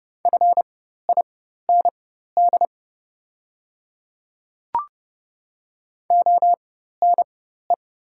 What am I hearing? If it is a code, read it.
Morse code, heard as a keyed tone, about 30 wpm, spaced as for 14 wpm, 700 Hz. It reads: FIND ONE
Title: 4x Character Spacing - Top 100 Words - 30wpm